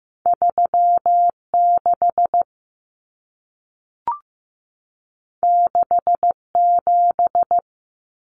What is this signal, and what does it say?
Morse code, heard as a keyed tone, about 15 wpm, 700 Hz. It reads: 3 6 6 7
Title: Numbers - 2-Digit 15wpm